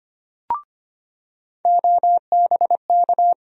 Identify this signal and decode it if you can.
Morse code, heard as a keyed tone, about 25 wpm, 700 Hz. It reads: OBK
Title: USA Call Sign Postfixes - 3 Letters 25wpm